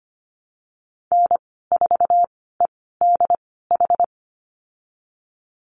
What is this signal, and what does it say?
Morse code, heard as a keyed tone, about 25 wpm, spaced as for 16 wpm, 700 Hz. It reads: N4EDH
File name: Lesson 15g - Number 5 - Call Signs 25wpm